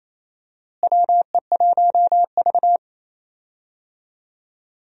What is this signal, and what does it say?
Morse code, heard as a keyed tone, about 27 wpm, 700 Hz. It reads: WE1V
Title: USA Call Signs - 4 Characters - Repeated 3x - 28wpm